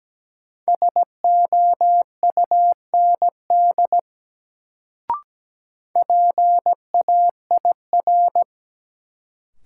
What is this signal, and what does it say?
Morse code, heard as a keyed tone, about 17 wpm, 700 Hz. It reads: SOUND PAIR
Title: Sets of 3 Words - New Word Drill - 17wpm